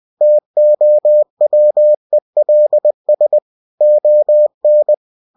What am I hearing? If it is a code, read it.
Morse code, heard as a keyed tone, about 20 wpm, 600 Hz. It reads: TOWELS ON
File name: cw-books / winnie-the-pooh / winnie-20wpm-page-0019